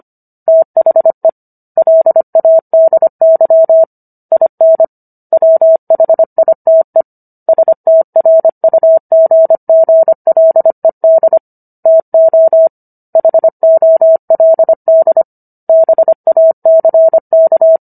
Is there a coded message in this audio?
Yes: Morse code from 0.5 to 17.8 s, heard as a keyed tone, about 25 wpm, 650 Hz. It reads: THE LADY IN WHITE STRUGGLED TO HOLD BACK